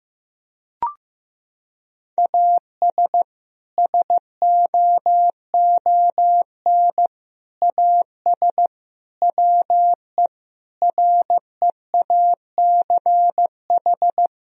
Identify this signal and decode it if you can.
Morse code, heard as a keyed tone, about 15 wpm, 700 Hz. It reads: AS SOON AS WE REACH
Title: Sentences from Top 1000 Words - Review Part 2 - 15wpm